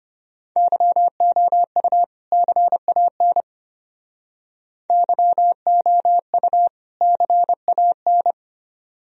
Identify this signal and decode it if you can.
Morse code, heard as a keyed tone, about 33 wpm, 700 Hz. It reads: YOU CAN YOU CAN